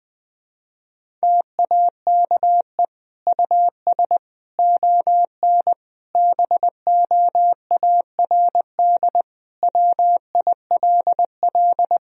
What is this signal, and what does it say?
Morse code, heard as a keyed tone, about 19 wpm, 700 Hz. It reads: TAKE US ON BOARD WILL
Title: Sentences from Top 1000 Words - Review Part 4 - 20wpm